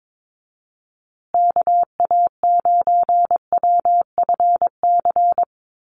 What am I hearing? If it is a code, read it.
Morse code, heard as a keyed tone, about 22 wpm, 700 Hz. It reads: KA9WFC
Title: USA Call Signs Repeated 2x 22wpm